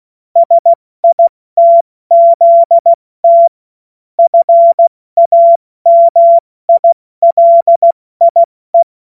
Morse code, heard as a keyed tone, about 16 wpm, 680 Hz. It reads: SITZT FAMILIE
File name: Morsekurs / Witze / Witz 25. April 70 F80